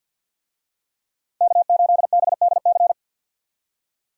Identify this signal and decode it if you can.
Morse code, heard as a keyed tone, about 53 wpm, 700 Hz. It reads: K8BDC
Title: USA Call Signs - 5 Characters - 50wpm